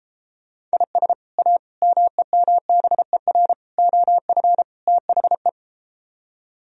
Morse code, heard as a keyed tone, about 34 wpm, 700 Hz. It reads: IS A MEMBER OF THE